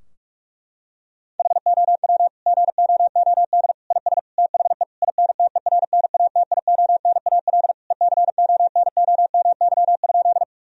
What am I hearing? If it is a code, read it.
Morse code, heard as a keyed tone, about 46 wpm, 700 Hz. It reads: HOW GOOD IS THE INTERNATIONAL ECONOMY?
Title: Sentences from Top 1000 Words - Review Part 4 - 45wpm